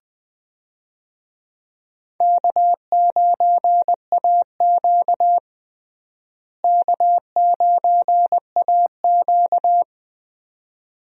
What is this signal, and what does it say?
Morse code, heard as a keyed tone, about 21 wpm, 700 Hz. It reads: K9AQ K9AQ
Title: USA Call Signs - 4 Characters - Repeated 2x - 20wpm